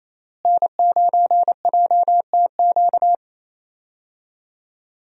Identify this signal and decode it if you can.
Morse code, heard as a keyed tone, about 28 wpm, 700 Hz. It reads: N9JTQ
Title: USA Call Signs - 5 Characters - Repeated 3x - 28wpm